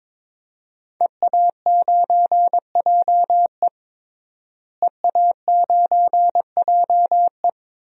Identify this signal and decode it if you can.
Morse code, heard as a keyed tone, about 21 wpm, 700 Hz. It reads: EA9JE EA9JE
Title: DXCC-100 Call Signs - 5 Characters - Repeated 2x - 22wpm